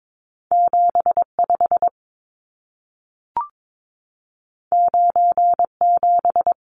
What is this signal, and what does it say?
Morse code, heard as a keyed tone, about 22 wpm, 700 Hz. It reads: 75 97
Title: Numbers - 2-Digit 22wpm